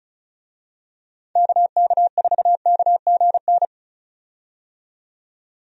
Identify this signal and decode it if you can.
Morse code, heard as a keyed tone, about 34 wpm, 700 Hz. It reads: KK4KGN
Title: USA Call Signs - 6 Characters - 35wpm